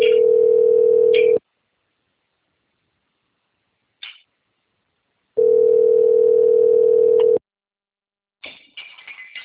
Ringback tone (North America), 440 + 480 Hz.